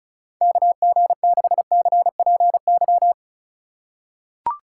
Morse code, heard as a keyed tone, about 34 wpm, 700 Hz. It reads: KG6CPY
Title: USA Call Signs - 2x Character Spacing - 35wpm